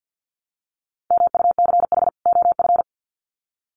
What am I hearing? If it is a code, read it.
Morse code, heard as a keyed tone, about 50 wpm, 700 Hz. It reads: MUCH OF
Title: Sets of 2 Words 50wpm